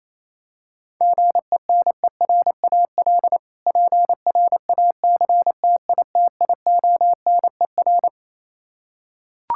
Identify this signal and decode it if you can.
Morse code, heard as a keyed tone, about 28 wpm, 700 Hz. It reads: GENERAL PRACTITIONER
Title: K1USN SST QSO Practice 28wpm